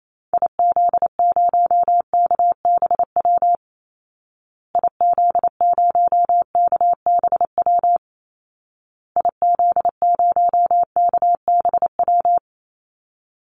Morse code, heard as a keyed tone, about 28 wpm, 700 Hz. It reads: IZ0KBW IZ0KBW IZ0KBW
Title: Call Signs Repeated 2x - 28wpm